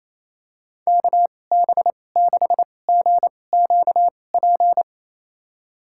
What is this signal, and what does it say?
Morse code, heard as a keyed tone, about 28 wpm, spaced as for 20 wpm, 700 Hz. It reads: KB6GQP
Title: USA Call Signs - 6 Characters - 2x Character Spacing - 28wpm